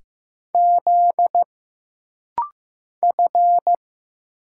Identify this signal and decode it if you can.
Morse code, heard as a keyed tone, about 15 wpm, 700 Hz. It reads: Z F
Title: Single Letter-Number Alt - Rapid-Fire 15wpm